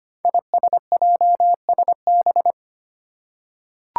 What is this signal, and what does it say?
Morse code, heard as a keyed tone, about 25 wpm, 700 Hz. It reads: ISJSB
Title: Five Challenging Letter-Number Combinations 25wpm